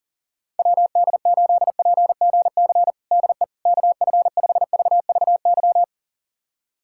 Morse code, heard as a keyed tone, about 40 wpm, 700 Hz. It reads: WD8PGC DE KF5VVY
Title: USA Callsign DE Callsign - 6 Characters - 40wpm